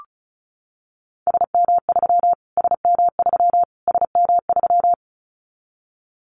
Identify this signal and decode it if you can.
Morse code, heard as a keyed tone, about 34 wpm, 700 Hz. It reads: SM3 SM3 SM3
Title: Callsign Prefixes for Top 25 DXCC - Repeated 3x - 35wpm